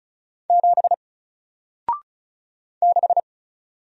Morse code, heard as a keyed tone, about 34 wpm, 700 Hz. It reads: 7 6
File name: Single-Letter-Number-35wpm